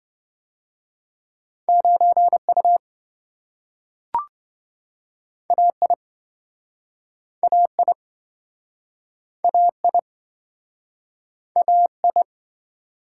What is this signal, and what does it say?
Morse code, heard as a keyed tone, about 27 wpm, 700 Hz. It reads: 9U AI AI AI AI